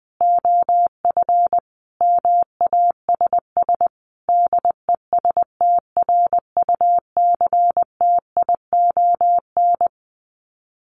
Morse code, heard as a keyed tone, about 20 wpm, 700 Hz. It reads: OF MASS DESTRUCTION
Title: Sets of 5 Words Encore 20wpm